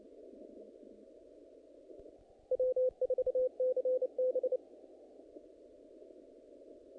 Morse code, heard as a keyed tone, about 29 wpm, 510 Hz. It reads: W4CB
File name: contests / naqp-cw-k5tr-op-k5pi- -2000